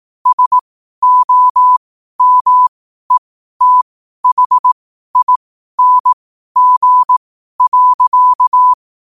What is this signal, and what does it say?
Morse code, heard as a keyed tone, about 18 wpm, spaced as for 12 wpm, 1000 Hz. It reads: SOMETHING.
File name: qotdfiles / 12wpm